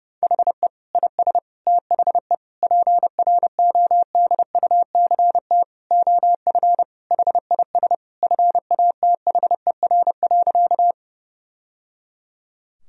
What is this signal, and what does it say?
Morse code, heard as a keyed tone, about 30 wpm, 700 Hz. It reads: HE IS THE PRODUCT OF HIS FATHER.
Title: Sentences from Top 1000 Words - Review Part 2 - 30wpm